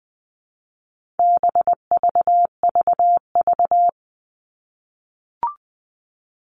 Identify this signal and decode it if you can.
Morse code, heard as a keyed tone, about 20 wpm, 700 Hz. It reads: BVVV
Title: All Cut Numbers - 4-Digit 20wpm